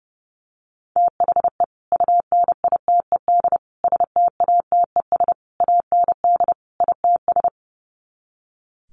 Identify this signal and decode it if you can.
Morse code, heard as a keyed tone, about 30 wpm, 700 Hz. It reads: THE UNITED STATES AND ITS
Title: Sets of 5 Words Encore 30wpm